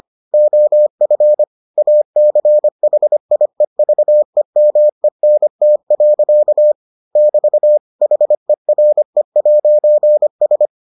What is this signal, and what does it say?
Morse code, heard as a keyed tone, about 25 wpm, 600 Hz. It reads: OF ACHIEVEMENT. <BT> HERE'S